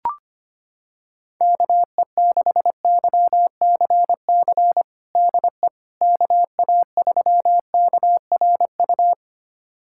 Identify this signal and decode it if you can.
Morse code, heard as a keyed tone, about 25 wpm, 700 Hz. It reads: KE6YCC DE KA3KRU
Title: USA Callsign DE Callsign 25wpm